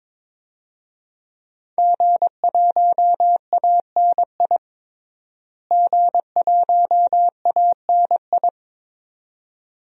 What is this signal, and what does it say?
Morse code, heard as a keyed tone, about 22 wpm, 700 Hz. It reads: G1ANI G1ANI
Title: DXCC-100 Call Signs - 5 Characters - Repeated 2x - 22wpm